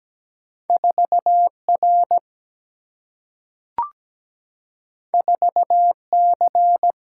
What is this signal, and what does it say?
Morse code, heard as a keyed tone, about 17 wpm, 700 Hz. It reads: 4 R 4 C
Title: Single-Letter Single-Number 17wpm